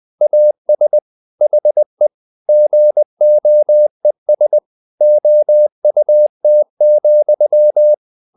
Morse code, heard as a keyed tone, about 20 wpm, 600 Hz. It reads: AS HE GOES OUT,
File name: the old man and the sea part 2 20wpm